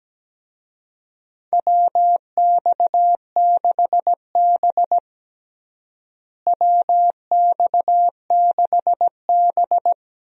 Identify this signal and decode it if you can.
Morse code, heard as a keyed tone, about 17 wpm, 700 Hz. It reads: WX6B WX6B
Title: USA Call Signs - 4 Characters - Repeated 2x - 17wpm